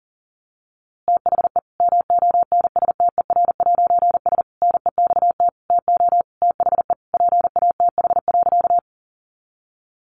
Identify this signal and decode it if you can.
Morse code, heard as a keyed tone, about 40 wpm, 700 Hz. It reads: THE MONSTER'S NEXT TO THE PATH.
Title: A Sound Of Thunder 40wpm